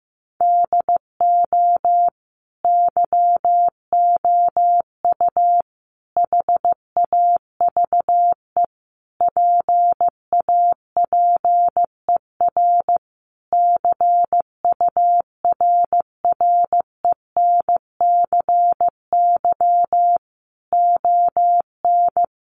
Morse code, heard as a keyed tone, about 15 wpm, 700 Hz. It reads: DO YOU HAVE PAPER CURRENCY ON
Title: Sentences from Top 1500 Words 15wpm